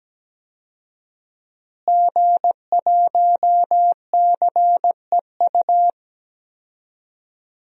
Morse code, heard as a keyed tone, about 17 wpm, 700 Hz. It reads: G1CEU
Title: DXCC-100 Call Signs - 5 Characters - Repeated 2x - 17wpm